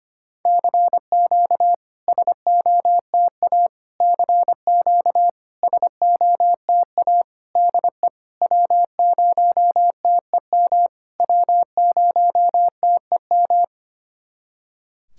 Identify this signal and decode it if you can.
Morse code, heard as a keyed tone, about 25 wpm, 700 Hz. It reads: CQ SOTA CQ SOTA DE W0TEM W0TEM
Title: SOTA QSO Practice 25wpm